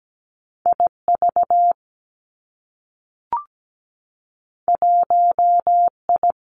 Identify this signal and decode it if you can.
Morse code, heard as a keyed tone, about 17 wpm, 700 Hz. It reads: IV 1I
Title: Two Challenging Letter-Number Combinations 17wpm